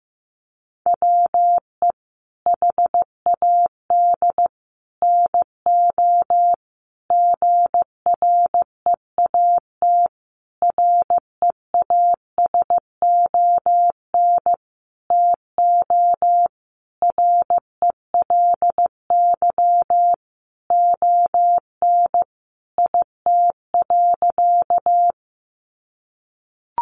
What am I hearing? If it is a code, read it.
Morse code, heard as a keyed tone, about 15 wpm, 700 Hz. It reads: WE HAD NO GREAT REASON TO RELY ON IT.
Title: Sentences from Top 1200 Words 15wpm